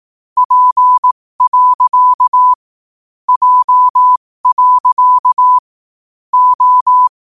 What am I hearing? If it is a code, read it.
Morse code, heard as a keyed tone, about 18 wpm, spaced as for 15 wpm, 1000 Hz. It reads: P. J. O
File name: qotdfiles / 15wpm